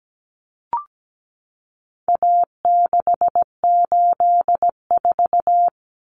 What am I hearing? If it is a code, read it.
Morse code, heard as a keyed tone, about 17 wpm, 700 Hz. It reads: A684
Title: Common Cut Numbers - 4-Digit 17wpm